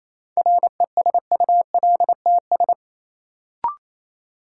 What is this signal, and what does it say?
Morse code, heard as a keyed tone, about 28 wpm, 700 Hz. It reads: RESULTS
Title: Sets of 4 Words - New Word Drill - 28wpm